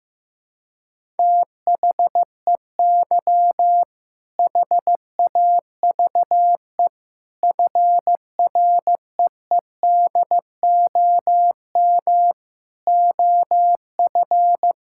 Morse code, heard as a keyed tone, about 15 wpm, 700 Hz. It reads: THEY HAVE FREEDOM OF
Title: Sentences from Top 1000 Words - Review Part 5 - 15wpm